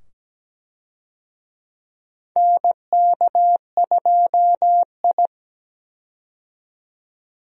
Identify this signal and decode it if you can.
Morse code, heard as a keyed tone, about 17 wpm, 700 Hz. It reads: NK2I
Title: USA Call Signs - 4 Characters - 17wpm